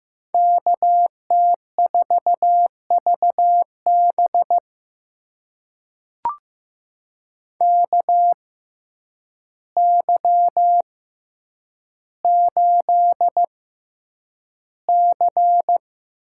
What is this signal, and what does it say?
Morse code, heard as a keyed tone, about 15 wpm, spaced as for 8 wpm, 700 Hz. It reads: KT4VB KY8C
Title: USA Call Signs - 6x Character Spacing - 15wpm